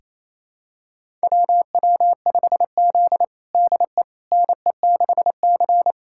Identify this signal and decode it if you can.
Morse code, heard as a keyed tone, about 28 wpm, 700 Hz. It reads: WW5Z DE NE6C
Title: USA Callsign DE Callsign - 4 Characters - 28wpm